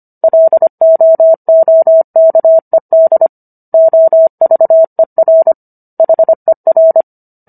Morse code, heard as a keyed tone, about 25 wpm, 650 Hz. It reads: LOOKED OVER HER